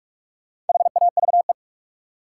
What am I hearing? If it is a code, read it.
Morse code, heard as a keyed tone, about 43 wpm, 700 Hz. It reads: HAVE
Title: Sentences from Top 100 Words - New Word Drill - 45wpm